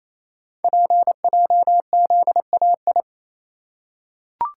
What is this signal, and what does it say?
Morse code, heard as a keyed tone, about 27 wpm, 700 Hz. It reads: PJZAI
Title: Code Groups - 5 Letters - 28wpm